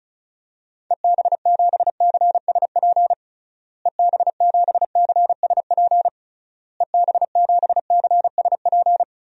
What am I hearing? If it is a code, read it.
Morse code, heard as a keyed tone, about 36 wpm, 700 Hz. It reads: EB7CSP EB7CSP EB7CSP
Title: DXCC-100 Call Signs - 6 Characters - Repeated 3x - 35wpm